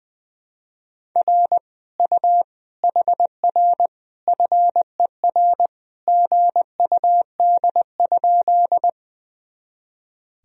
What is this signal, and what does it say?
Morse code, heard as a keyed tone, about 20 wpm, 700 Hz. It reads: R U HR FER GUD?
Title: Sentences from Top 200 Words and Abbreviations 20wpm